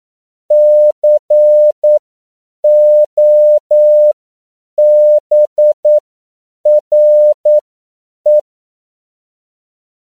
Morse code, heard as a keyed tone, about 9 wpm, spaced as for 7 wpm, 600 Hz. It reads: COBRE
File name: CW / 10PPM